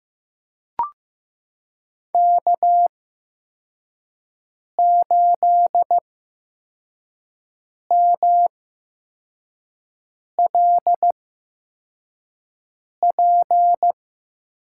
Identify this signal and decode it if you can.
Morse code, heard as a keyed tone, about 15 wpm, spaced as for 4 wpm, 700 Hz. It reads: K8MLP
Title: USA Call Signs - 5 Characters - 8x Character Spacing - 15wpm